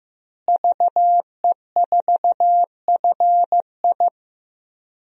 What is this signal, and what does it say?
Morse code, heard as a keyed tone, about 15 wpm, 700 Hz. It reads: VE4FI
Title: DXCC-100 Call Signs - 5 Characters - Repeated 3x - 15wpm